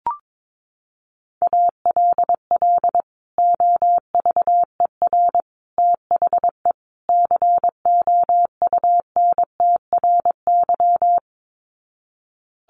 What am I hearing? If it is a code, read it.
Morse code, heard as a keyed tone, about 22 wpm, 700 Hz. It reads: ALL OVER THE COUNTRY
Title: Sets of 4 Words 22wpm